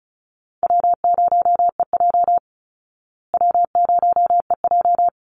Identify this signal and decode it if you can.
Morse code, heard as a keyed tone, about 34 wpm, 700 Hz. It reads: W0EJ W0EJ
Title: Easy Call Signs Repeated 2x - 35wpm